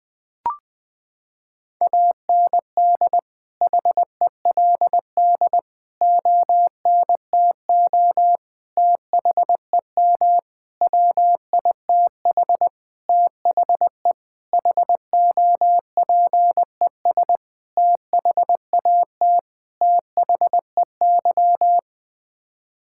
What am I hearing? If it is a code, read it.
Morse code, heard as a keyed tone, about 20 wpm, 700 Hz. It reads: AND HELD ONTO THEM WITH THE HOPES THAT THEY
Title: Sets of 9 Words - 20wpm